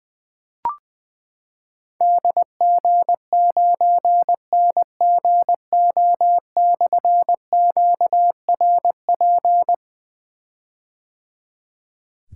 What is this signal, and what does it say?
Morse code, heard as a keyed tone, about 20 wpm, 700 Hz. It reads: DG9NGO/QRP
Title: Difficult Call Signs 20wpm